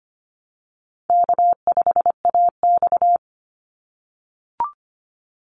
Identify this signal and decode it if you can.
Morse code, heard as a keyed tone, about 25 wpm, 700 Hz. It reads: K5AX
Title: Easy Call Signs - 25wpm